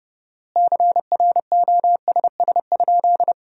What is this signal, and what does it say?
Morse code, heard as a keyed tone, about 32 wpm, 700 Hz. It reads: CROSS?
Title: 4x Word-Spacing - Sentences from Top 300 Words - 30wpm